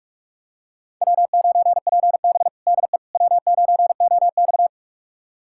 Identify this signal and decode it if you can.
Morse code, heard as a keyed tone, about 43 wpm, 700 Hz. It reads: W9PB DE W9OX
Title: USA Callsign DE Callsign - 4 Characters - 45wpm